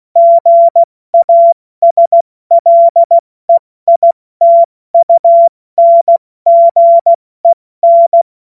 Morse code, heard as a keyed tone, about 16 wpm, 680 Hz. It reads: GASLEITUNGEN
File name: Morsekurs / Witze / Witz 02. Dezember 2025 70 F80